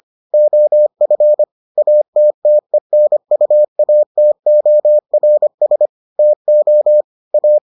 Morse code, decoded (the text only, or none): OF ATTENUATORS TO A